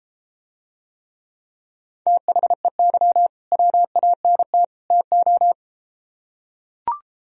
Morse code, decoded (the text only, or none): THEY WANT TO